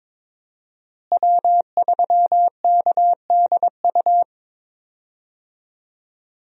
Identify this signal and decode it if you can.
Morse code, heard as a keyed tone, about 22 wpm, 700 Hz. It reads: W3KDU
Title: USA Call Signs - 5 Characters - Repeated 3x - 22wpm